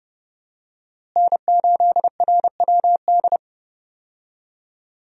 Morse code, heard as a keyed tone, about 30 wpm, 700 Hz. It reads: N8RWD
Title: USA Call Signs - 5 Characters - Repeated 2x - 30wpm